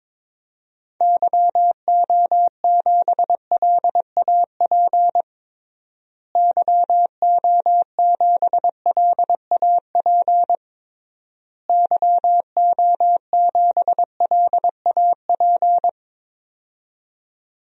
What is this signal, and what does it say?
Morse code, heard as a keyed tone, about 22 wpm, 700 Hz. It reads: YO7LAP YO7LAP YO7LAP
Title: DXCC-100 Call Signs - 6 Characters - Repeated 3x - 22wpm